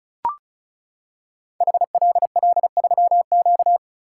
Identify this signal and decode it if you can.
Morse code, heard as a keyed tone, about 36 wpm, 690 Hz. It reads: HLL3Q
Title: Code Groups - 5 Letter:Numbers - 35wpm